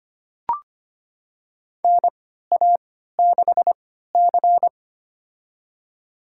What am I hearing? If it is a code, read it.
Morse code, heard as a keyed tone, about 25 wpm, spaced as for 14 wpm, 700 Hz. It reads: NA6C
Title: USA Call Signs - 4 Characters - 3x Character Spacing - 25wpm